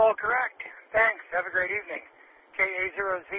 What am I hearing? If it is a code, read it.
Emergency Alert System (SAME) tones, heard as harsh buzzing bursts, with speech. It is heard from 0.4 to 2.9 s.